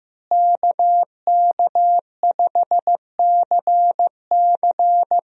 Morse code, heard as a keyed tone, about 16 wpm, 700 Hz. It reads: KK5CC
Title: USA Call Signs - 5 Characters - Repeated 3x - 15wpm